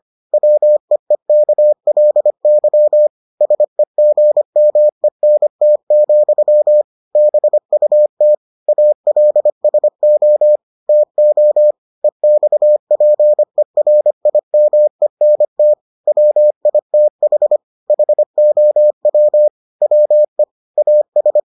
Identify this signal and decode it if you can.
Morse code, heard as a keyed tone, about 25 wpm, 600 Hz. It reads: WEEKLY SEGMENT, BUT ALSO TO EXPERIMENT WITH HOW WE AS